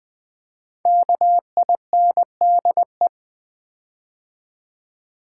Morse code, heard as a keyed tone, about 19 wpm, 700 Hz. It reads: KINDE
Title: Build A Word - 5 Letters At A Time - 12-Letter Words - 20wpm